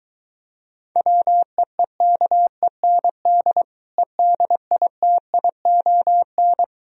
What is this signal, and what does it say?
Morse code, heard as a keyed tone, about 23 wpm, 700 Hz. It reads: WEEKEND EDITION